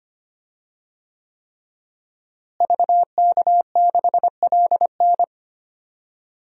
Morse code, heard as a keyed tone, about 25 wpm, 700 Hz. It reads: VK6LN
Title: DXCC-100 Call Signs - 5 Characters - Repeated 3x - 25wpm